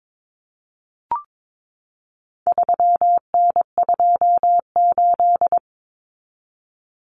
Morse code, heard as a keyed tone, about 22 wpm, 700 Hz. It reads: 3N28
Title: Common Cut Numbers - 4-Digit 22wpm